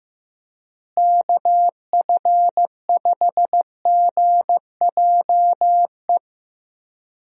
Morse code, heard as a keyed tone, about 14 wpm, 700 Hz. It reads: KF5GJE
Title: USA Call Signs - 6 Characters - Repeated 3x - 15wpm